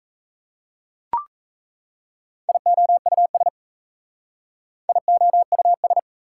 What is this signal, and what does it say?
Morse code, heard as a keyed tone, about 37 wpm, 690 Hz. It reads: IOUS IOUS